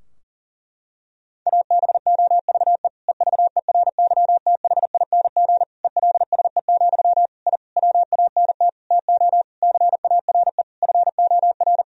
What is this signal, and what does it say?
Morse code, heard as a keyed tone, about 40 wpm, 700 Hz. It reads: ABOVE EVERYTHING ELSE, I WANT TO CARE FOR
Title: Sentences from Top 1000 Words - Review Part 2 - 40wpm